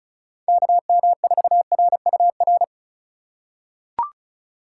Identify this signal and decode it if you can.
Morse code, heard as a keyed tone, about 36 wpm, 700 Hz. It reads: KM4RUR
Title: USA Call Signs - 6 Characters - 3x Character Spacing - 35wpm